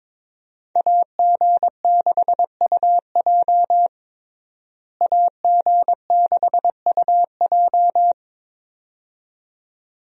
Morse code, heard as a keyed tone, about 22 wpm, 700 Hz. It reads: AG6UJ AG6UJ
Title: USA Call Signs - 5 Characters - Repeated 2x - 22wpm